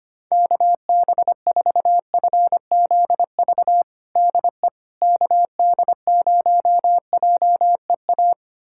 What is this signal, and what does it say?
Morse code, heard as a keyed tone, about 25 wpm, 700 Hz. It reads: KB4FZV DE KD0JEA
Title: USA Callsign DE Callsign - 6 Characters - 25wpm